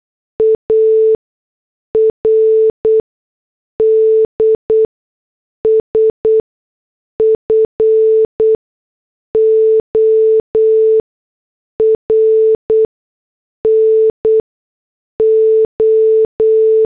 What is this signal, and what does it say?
Morse code, heard as a keyed tone, about 8 wpm, spaced as for 6 wpm, 440 Hz. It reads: ARDS FOR NO